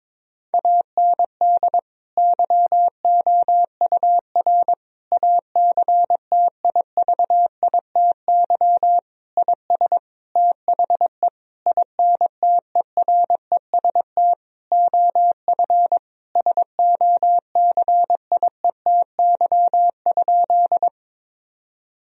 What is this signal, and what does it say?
Morse code, heard as a keyed tone, about 22 wpm, 700 Hz. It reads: AND YOUR ACTIVITY IS THE INTEREST OF SOCIETY?
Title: Sentences from Top 1000 Words - Review Part 4 - 22wpm